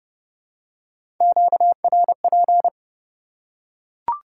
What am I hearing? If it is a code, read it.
Morse code, heard as a keyed tone, about 29 wpm, 700 Hz. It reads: QRP